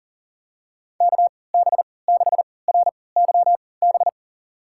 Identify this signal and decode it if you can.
Morse code, heard as a keyed tone, about 40 wpm, spaced as for 23 wpm, 700 Hz. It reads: KB6RYB
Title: USA Call Signs - 6 Characters - 3x Character Spacing - 40wpm